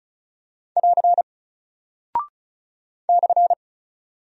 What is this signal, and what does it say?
Morse code, heard as a keyed tone, about 36 wpm, 700 Hz. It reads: <AR> /
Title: Single Character (Letter:Number:Punctuation:Prosign) Alt 35wpm